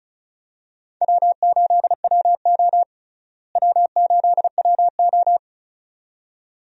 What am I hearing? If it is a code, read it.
Morse code, heard as a keyed tone, about 36 wpm, 700 Hz. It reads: W8WO W8WO
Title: USA Call Signs - 4 Characters - Repeated 2x - 35wpm